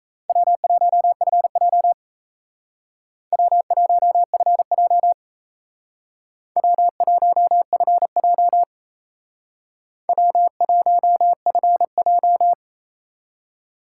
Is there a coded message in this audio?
Yes: Morse code from 0.3 to 12.5 s, heard as a keyed tone, about 36 wpm, 700 Hz. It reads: W1FJ W1FJ W1FJ W1FJ